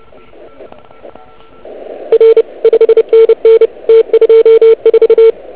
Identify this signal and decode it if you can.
Morse code, heard as a keyed tone, about 29 wpm, 440 Hz. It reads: R 5NN T24